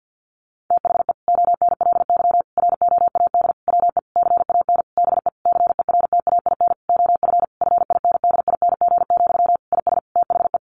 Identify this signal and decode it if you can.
Morse code, heard as a keyed tone, about 50 wpm, 700 Hz. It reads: THE ONLY ROAD WE CAN BE CERTAIN OF FINDING, IS THE
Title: Sentences from Top 1100 Words 50wpm